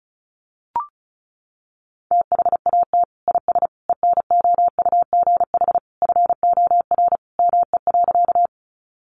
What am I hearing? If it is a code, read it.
Morse code, heard as a keyed tone, about 34 wpm, 700 Hz. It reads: THAT IS ENOUGH FOR ME.
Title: Sentences from Top 400 Words 35wpm